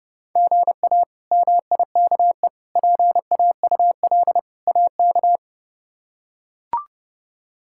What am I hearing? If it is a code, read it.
Morse code, heard as a keyed tone, about 30 wpm, 700 Hz. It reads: GA MIKE PAUL AK
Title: K1USN SST QSO Practice - 30wpm